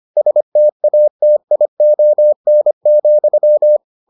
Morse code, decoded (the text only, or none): STATION,